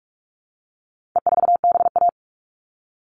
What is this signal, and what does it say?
Morse code, heard as a keyed tone, about 44 wpm, 700 Hz. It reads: E4BA
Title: All Cut Numbers - 4-Digit 45wpm